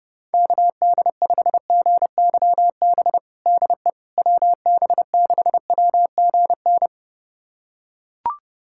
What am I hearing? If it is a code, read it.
Morse code, heard as a keyed tone, about 32 wpm, 700 Hz. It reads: KD5GYB DE WB6WGN
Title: USA Callsign DE Callsign - 6 Characters - 30wpm